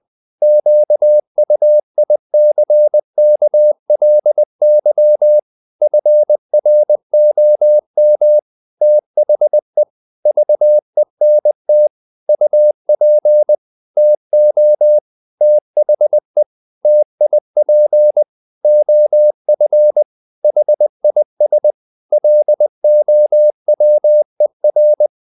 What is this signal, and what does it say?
Morse code, heard as a keyed tone, about 20 wpm, 600 Hz. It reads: QUICKLY FROM THE VENT UP TO THE TIP OF HIS LOWER
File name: the old man and the sea part 2 20wpm